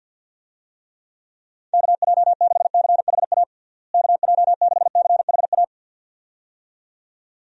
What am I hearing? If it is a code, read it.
Morse code, heard as a keyed tone, about 50 wpm, 700 Hz. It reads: KJ6CHA KJ6CHA
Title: USA Call Signs - 6 Characters - Repeated 2x - 50wpm